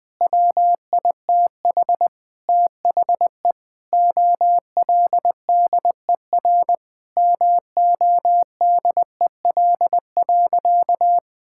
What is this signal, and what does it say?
Morse code, heard as a keyed tone, about 20 wpm, 700 Hz. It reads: WITH THE OLDER MODEL.
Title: Sentences with 7-Word Phrases - 20wpm